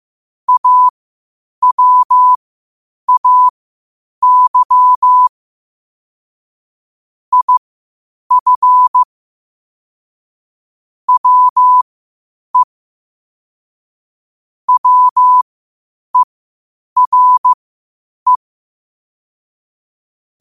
Morse code, heard as a keyed tone, about 15 wpm, spaced as for 8 wpm, 1000 Hz. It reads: AWAY IF WE WERE
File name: qotdfiles / 8wpm